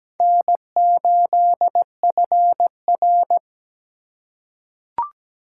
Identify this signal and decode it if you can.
Morse code, heard as a keyed tone, about 17 wpm, 700 Hz. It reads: N8FR
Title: USA Call Signs - 4 Characters - 4x Character Spacing - 17wpm